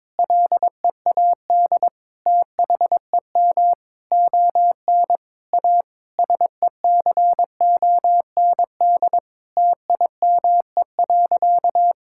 Morse code, heard as a keyed tone, about 22 wpm, 700 Hz. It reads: LEAD THEM ON A SECOND TIME.
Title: Sentences from Top 1000 Words - Review Part 5 - 22wpm